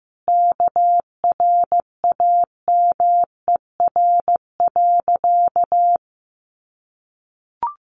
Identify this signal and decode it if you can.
Morse code, heard as a keyed tone, about 15 wpm, 700 Hz. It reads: KRAMER.
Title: A Sound Of Thunder 15wpm